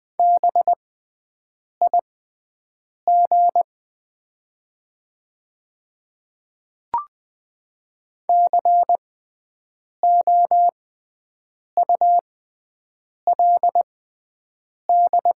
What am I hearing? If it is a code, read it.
Morse code, heard as a keyed tone, about 20 wpm, spaced as for 7 wpm, 700 Hz. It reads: BIG COULD
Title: 6x Character Spacing - Top 200 Words - 20wpm